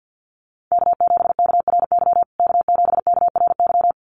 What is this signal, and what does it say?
Morse code, heard as a keyed tone, about 50 wpm, 700 Hz. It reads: K7KRY K7KRY
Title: USA Call Signs Repeated 2x 50wpm